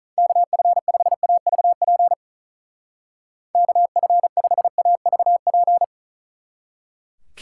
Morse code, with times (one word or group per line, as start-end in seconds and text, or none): 0.2-2.1 KF5AVP
3.5-5.8 KF5AVP